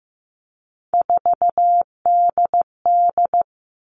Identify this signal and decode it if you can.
Morse code, heard as a keyed tone, about 15 wpm, 700 Hz. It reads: 4DD
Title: All Cut Numbers - 3-Digit 15wpm